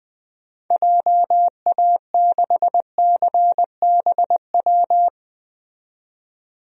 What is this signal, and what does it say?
Morse code, heard as a keyed tone, about 20 wpm, 700 Hz. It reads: JA6CBW
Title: DXCC-100 Call Signs - 6 Characters - 20wpm